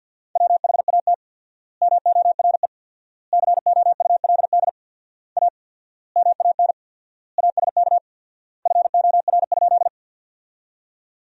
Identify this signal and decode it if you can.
Morse code, heard as a keyed tone, about 52 wpm, 700 Hz. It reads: WHAT MORE COULD A MAN ASK FOR?
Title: 4x Word-Spacing - Sentences from Top 200 Words - 50wpm